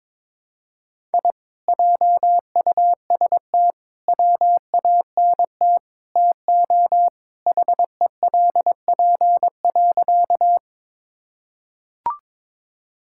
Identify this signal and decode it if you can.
Morse code, heard as a keyed tone, about 22 wpm, 700 Hz. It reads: I JUST WANT TO HELP.
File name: practice / 22wpm / Sentences with 4-Word Phrases - 22wpm